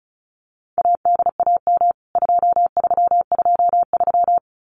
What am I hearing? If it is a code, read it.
Morse code, heard as a keyed tone, about 34 wpm, 700 Hz. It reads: ADAM 2323
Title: CWT Exchange 35wpm